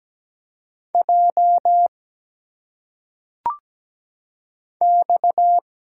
Morse code, heard as a keyed tone, about 17 wpm, 700 Hz. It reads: J X
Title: Single Letters - Phonetic Pronunciation 17wpm